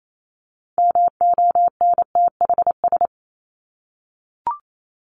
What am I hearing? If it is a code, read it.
Morse code, heard as a keyed tone, about 28 wpm, 700 Hz. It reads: MONTHS
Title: ES:S Suffix - Top 100 Words with Base Words 28wpm